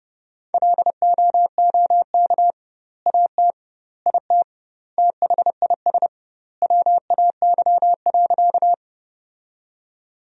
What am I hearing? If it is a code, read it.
Morse code, heard as a keyed tone, about 31 wpm, 700 Hz. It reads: LOOK AT IT THIS WAY.
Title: 2x Word-Spacing - Sentences from Top 100 Words - 30wpm